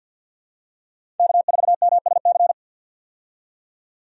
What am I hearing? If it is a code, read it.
Morse code, heard as a keyed tone, about 51 wpm, 700 Hz. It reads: K4MSC
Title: USA Call Signs - 5 Characters - 50wpm